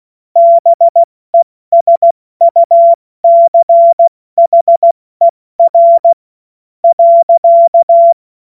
Morse code, heard as a keyed tone, about 16 wpm, 680 Hz. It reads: BESUCHER .
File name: Morsekurs / Witze / Witz 10. Mai 2022 70 F80